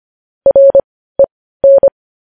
Morse code, heard as a keyed tone, about 25 wpm, spaced as for 15 wpm, 550 Hz. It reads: REN
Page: News Headlines in Morse Code at 15 WPM